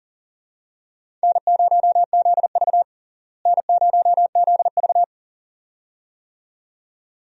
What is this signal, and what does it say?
Morse code, heard as a keyed tone, about 40 wpm, 700 Hz. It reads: N0ZV N0ZV
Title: USA Call Signs - 4 Characters - Repeated 2x - 40wpm